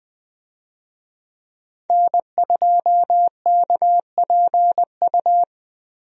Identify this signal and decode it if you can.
Morse code, heard as a keyed tone, about 20 wpm, 700 Hz. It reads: N2KPU
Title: USA Call Signs - 5 Characters - 20wpm